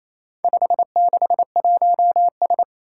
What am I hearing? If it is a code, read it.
Morse code, heard as a keyed tone, about 27 wpm, 700 Hz. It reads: 561S
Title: Four Challenging Letter-Number Combinations 28wpm